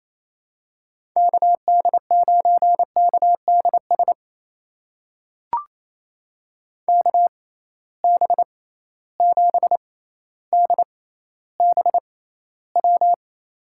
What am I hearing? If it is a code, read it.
Morse code, heard as a keyed tone, about 28 wpm, spaced as for 14 wpm, 700 Hz. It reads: KD9KDS KB7DBW
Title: USA Call Signs - 6 Characters - 6x Character Spacing - 28wpm